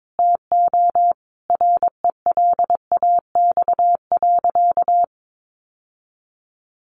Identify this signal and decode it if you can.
Morse code, heard as a keyed tone, about 22 wpm, 700 Hz. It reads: TO RELAX.
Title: Sentences from Top 1400 Words 22wpm